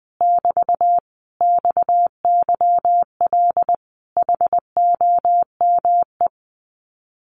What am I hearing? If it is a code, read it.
Morse code, heard as a keyed tone, about 20 wpm, 700 Hz. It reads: <BT> XYL HOME
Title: Common QSO Phrases Encore 20wpm